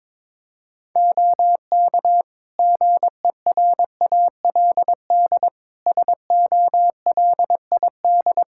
Morse code, heard as a keyed tone, about 22 wpm, 700 Hz. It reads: OK GERALD SOLID
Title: Practice QSO - Exchange 3 - More Info - 22wpm